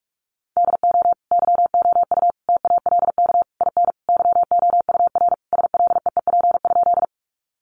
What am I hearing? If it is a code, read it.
Morse code, heard as a keyed tone, about 44 wpm, 700 Hz. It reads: DO YOU TALK IN YOUR SLEEP?
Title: Sentences from Top 500 Words 45wpm